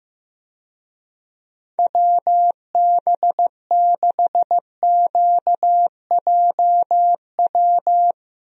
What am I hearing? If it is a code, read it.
Morse code, heard as a keyed tone, about 15 wpm, 700 Hz. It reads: WB6QJW
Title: USA Call Signs - 6 Characters - Repeated 3x - 15wpm